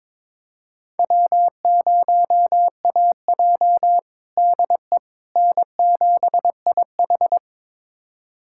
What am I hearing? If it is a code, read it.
Morse code, heard as a keyed tone, about 21 wpm, 700 Hz. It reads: W0AJ DE N7IH
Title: USA Callsign DE Callsign - 4 Characters - 22wpm